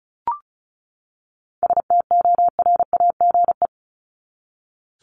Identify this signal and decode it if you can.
Morse code, heard as a keyed tone, about 34 wpm, 700 Hz. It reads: STORAGE
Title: Top 1251-1500 Words 35wpm